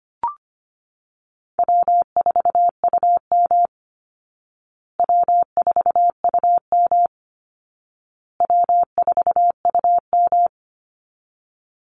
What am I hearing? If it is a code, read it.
Morse code, heard as a keyed tone, about 25 wpm, 700 Hz. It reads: W4UM W4UM W4UM
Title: Easy Call Signs Repeated 3x - 25wpm